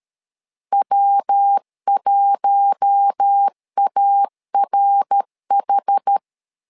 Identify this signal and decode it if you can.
Morse code, heard as a keyed tone, about 13 wpm, 800 Hz. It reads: W1ARH
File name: LEZIONI / 4 Q1 12wpm 2028